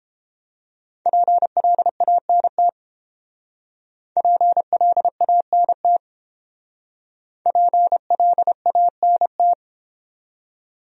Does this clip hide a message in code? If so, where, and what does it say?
Yes: Morse code from 1.1 to 9.5 s, heard as a keyed tone, about 29 wpm, 700 Hz. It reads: PLANT PLANT PLANT